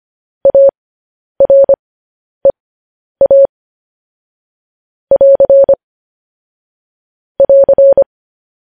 Morse code, heard as a keyed tone, about 25 wpm, spaced as for 10 wpm, 550 Hz. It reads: AREA <AR> <AR>
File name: podcast / News headlines-1x-25x10